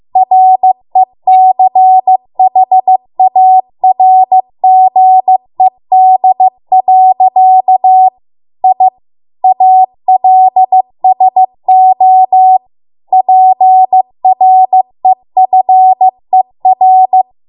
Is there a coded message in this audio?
Yes: Morse code from 0.2 to 17.2 s, heard as a keyed tone, about 15 wpm, 750 Hz. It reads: RECHARGED. I ALSO PREFER